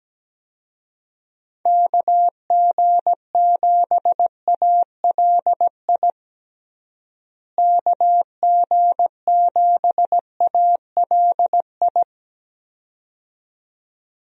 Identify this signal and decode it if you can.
Morse code, heard as a keyed tone, about 17 wpm, 700 Hz. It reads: KG7ALI KG7ALI
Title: USA Call Signs - 6 Characters - Repeated 2x - 17wpm